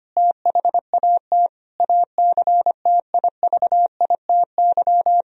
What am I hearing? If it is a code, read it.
Morse code, heard as a keyed tone, about 25 wpm, 700 Hz. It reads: THAT ACTIVITY
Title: Sentences from Top 1000 Words - Review Part 4 - 25wpm